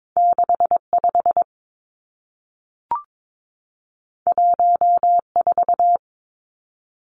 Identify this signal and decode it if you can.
Morse code, heard as a keyed tone, about 22 wpm, 700 Hz. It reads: 6 5 1 4
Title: Two Challenging Letter-Number Combinations 22wpm